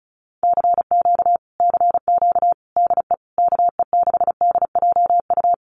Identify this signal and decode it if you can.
Morse code, heard as a keyed tone, about 36 wpm, 700 Hz. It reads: CQ CQ DE KE6DJU
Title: Common QSO Phrases Encore 35wpm